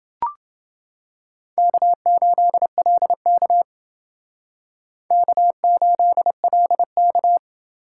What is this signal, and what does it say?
Morse code, heard as a keyed tone, about 28 wpm, 700 Hz. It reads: K8LK K8LK